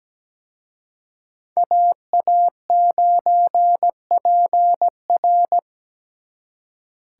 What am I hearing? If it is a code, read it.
Morse code, heard as a keyed tone, about 20 wpm, 700 Hz. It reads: AA9PR
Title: Speed Racing - USA 2x2 Call Signs 17wpm